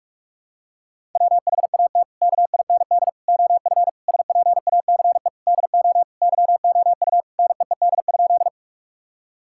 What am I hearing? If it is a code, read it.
Morse code, heard as a keyed tone, about 42 wpm, 700 Hz. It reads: WHAT KIND OF SPACE DO YOU NEED?
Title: Sentences from Top 1000 Words - Review Part 5 - 45wpm